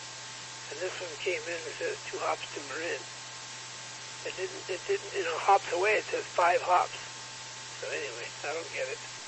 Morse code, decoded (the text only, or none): none